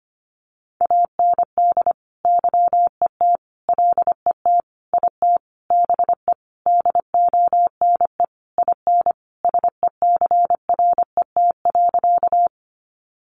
Morse code, heard as a keyed tone, about 25 wpm, 700 Hz. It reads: AND YET LET IT BE DONE IN SECRET.
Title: Sentences from Top 1200 Words 25wpm